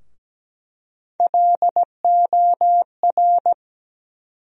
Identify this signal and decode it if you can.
Morse code, heard as a keyed tone, about 17 wpm, 700 Hz. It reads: LOR
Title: USA Call Sign Postfixes - 3 Letters 17wpm